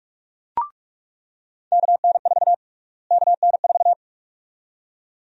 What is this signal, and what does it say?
Morse code, heard as a keyed tone, about 47 wpm, 700 Hz. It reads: KN4 KN4
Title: USA Call Sign Prefixes - Repeated 2x - 45wpm